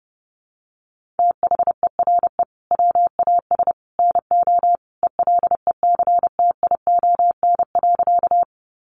Morse code, heard as a keyed tone, about 30 wpm, 700 Hz. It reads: THERE WAS NO ELECTION.
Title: Sentences from Top 1500 Words 30wpm